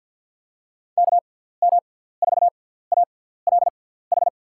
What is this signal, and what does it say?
Morse code, heard as a keyed tone, about 53 wpm, spaced as for 17 wpm, 700 Hz. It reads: KM4ALH
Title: USA Call Signs - 6 Characters - 6x Character Spacing - 50wpm